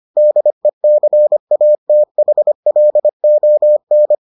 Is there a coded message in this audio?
Yes: Morse code from 0.2 to 4.1 s, heard as a keyed tone, about 25 wpm, 600 Hz. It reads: DECATHLON